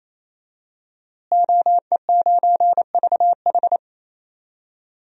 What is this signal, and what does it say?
Morse code, heard as a keyed tone, about 28 wpm, 700 Hz. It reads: OE9VH
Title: DXCC-100 Call Signs - 5 Characters - Repeated 3x - 28wpm